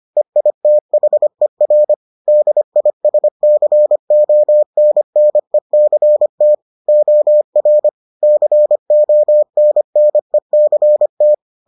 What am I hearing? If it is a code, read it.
Morse code, heard as a keyed tone, about 25 wpm, 600 Hz. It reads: EITHER DISCONNECT OR CONNECT